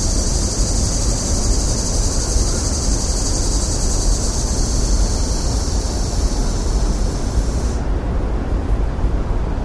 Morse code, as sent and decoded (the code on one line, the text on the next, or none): none